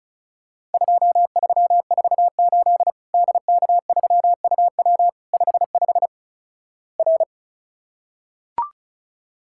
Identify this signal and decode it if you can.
Morse code, heard as a keyed tone, about 34 wpm, 700 Hz. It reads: 2348 DK3UW 55 R
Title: Worked All Europe DX Contest - QTC Exchange - 35wpm